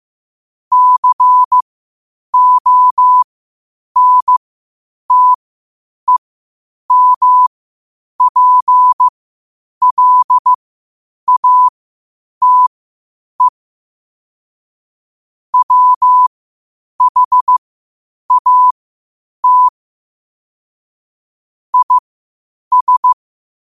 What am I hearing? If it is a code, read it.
Morse code, heard as a keyed tone, about 15 wpm, spaced as for 8 wpm, 1000 Hz. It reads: CONTEMPLATE WHAT IS
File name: qotdfiles / 8wpm